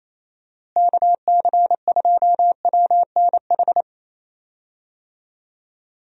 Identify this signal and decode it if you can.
Morse code, heard as a keyed tone, about 28 wpm, 700 Hz. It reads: KC2WNH
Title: USA Call Signs - 6 Characters - Repeated 3x - 28wpm